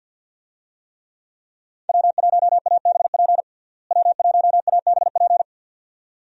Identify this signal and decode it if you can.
Morse code, heard as a keyed tone, about 52 wpm, 700 Hz. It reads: W1ABP W1ABP
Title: USA Call Signs - 5 Characters - Repeated 2x - 50wpm